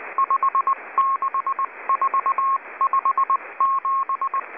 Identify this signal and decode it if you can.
Morse code, heard as a keyed tone, about 19 wpm, 1080 Hz, with radio noise. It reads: 56457